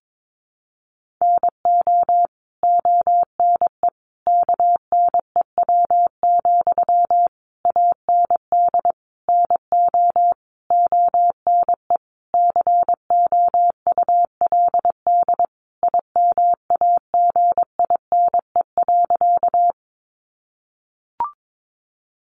Morse code, heard as a keyed tone, about 22 wpm, 700 Hz. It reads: NO ONE KNEW, AND NO ONE COULD IMAGINE.
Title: Sentences from Top 1400 Words 22wpm